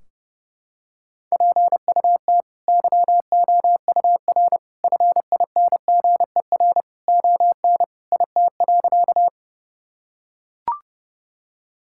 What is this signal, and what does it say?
Morse code, heard as a keyed tone, about 30 wpm, 700 Hz. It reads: PUT YOUR FINGER ON IT.
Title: Sentences from Top 1600 Words 30wpm